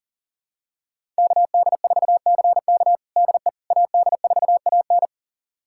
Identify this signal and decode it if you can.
Morse code, heard as a keyed tone, about 40 wpm, 700 Hz. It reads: KD4CK DE AD4AN
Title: USA Callsign DE Callsign - 5 Characters - 40wpm